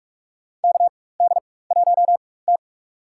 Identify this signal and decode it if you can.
Morse code, heard as a keyed tone, about 46 wpm, spaced as for 21 wpm, 700 Hz. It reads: KD1T
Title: USA Call Signs - 4 Characters - 4x Character Spacing - 45wpm